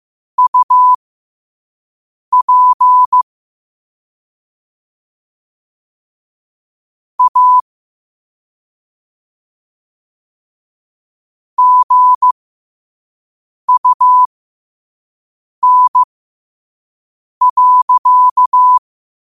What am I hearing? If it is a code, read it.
Morse code, heard as a keyed tone, about 15 wpm, spaced as for 5 wpm, 1000 Hz. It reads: UP A GUN.
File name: qotdfiles / 5wpm